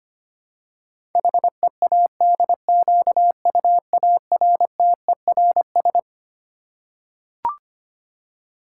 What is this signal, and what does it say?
Morse code, heard as a keyed tone, about 25 wpm, 700 Hz. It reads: HEADQUARTERS
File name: Twelve Letter Words - 25wpm